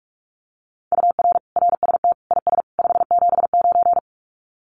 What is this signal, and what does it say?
Morse code, heard as a keyed tone, about 44 wpm, 700 Hz. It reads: UR RST IS 579
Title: Common QSO Phrases 45wpm